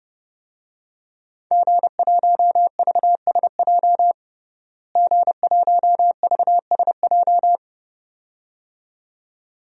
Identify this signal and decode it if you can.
Morse code, heard as a keyed tone, about 32 wpm, 700 Hz. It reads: G1VSJ G1VSJ
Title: DXCC-100 Call Signs - 5 Characters - Repeated 2x - 30wpm